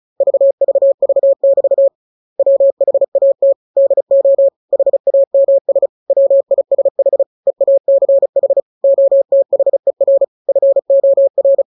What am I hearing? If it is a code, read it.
Morse code, heard as a keyed tone, about 35 wpm, 550 Hz. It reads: VVV<BT> WHAT DO HAMS WISH EACH OTHER FOR
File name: jokes / joke 17 35 0000